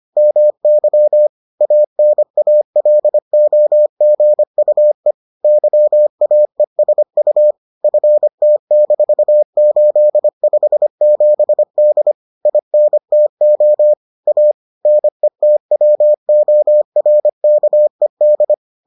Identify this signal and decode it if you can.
Morse code, heard as a keyed tone, about 25 wpm, 600 Hz. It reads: MY ANALOGUE YAESU FT-857D INTO A NETWORKED